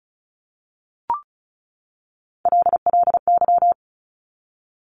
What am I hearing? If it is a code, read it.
Morse code, heard as a keyed tone, about 36 wpm, 700 Hz. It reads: LLY
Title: Top Three Letter Combinations in Words 35wpm